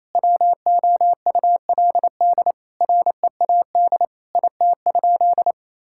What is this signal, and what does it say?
Morse code, heard as a keyed tone, about 27 wpm, 700 Hz. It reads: WOULD READ IT?
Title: Sentences from Top 1600 Words 28wpm